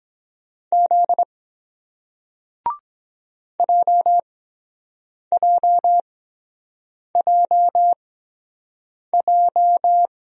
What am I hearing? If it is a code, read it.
Morse code, heard as a keyed tone, about 23 wpm, spaced as for 7 wpm, 700 Hz. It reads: Z JJJJ